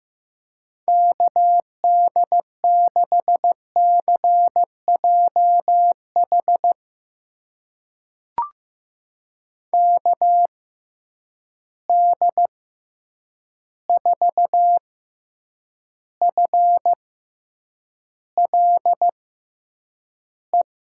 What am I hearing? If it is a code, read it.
Morse code, heard as a keyed tone, about 15 wpm, spaced as for 8 wpm, 700 Hz. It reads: KD6CJH KD4FLE
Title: USA Call Signs - 6 Characters - 6x Character Spacing - 15wpm